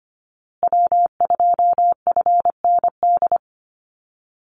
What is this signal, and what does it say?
Morse code, heard as a keyed tone, about 25 wpm, 700 Hz. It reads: W2FND
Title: USA Call Signs Repeated 2x 25wpm